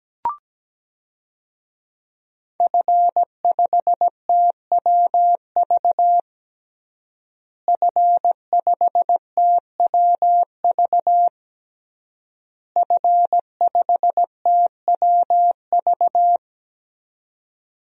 Morse code, heard as a keyed tone, about 17 wpm, 700 Hz. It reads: F5TWV F5TWV F5TWV
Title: DXCC-100 Call Signs - 5 Characters - Repeated 3x - 17wpm